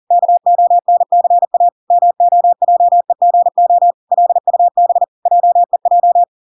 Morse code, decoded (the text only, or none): KONCA MOJEGO LUB JEJ